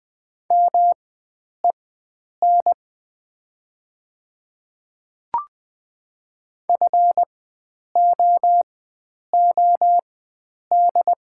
Morse code, heard as a keyed tone, about 20 wpm, spaced as for 9 wpm, 700 Hz. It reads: MEN FOOD
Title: 4x Character Spacing - Top 300 Words - 20wpm